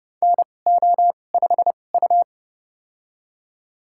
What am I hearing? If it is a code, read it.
Morse code, heard as a keyed tone, about 31 wpm, spaced as for 22 wpm, 700 Hz. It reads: NO5U
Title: USA Call Signs - 4 Characters - 2x Character Spacing - 30wpm